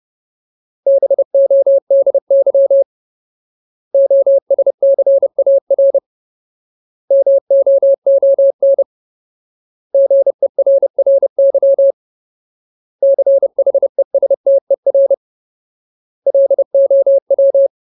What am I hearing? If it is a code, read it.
Morse code, heard as a keyed tone, about 30 wpm, 550 Hz. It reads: BODY OSCAR MOON GERRY CHESTER LOW
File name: Ultimate Word List / 30WPM Ultimate